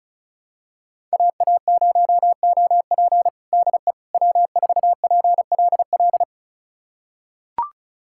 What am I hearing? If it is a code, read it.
Morse code, heard as a keyed tone, about 36 wpm, 700 Hz. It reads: AA0OP DE W4PLL
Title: USA Callsign DE Callsign - 5 Characters - 35wpm